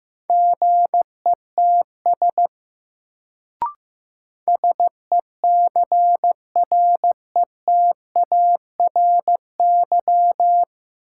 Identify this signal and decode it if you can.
Morse code, heard as a keyed tone, about 15 wpm, 700 Hz. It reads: GETS SECRETARY
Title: Sentences from Top 1300 Words - New Word Drill - 15wpm